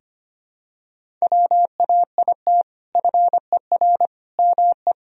Morse code, heard as a keyed tone, about 25 wpm, 700 Hz. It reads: WAIT FER ME
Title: Sentences from Top 500 Words and Abbreviations 25wpm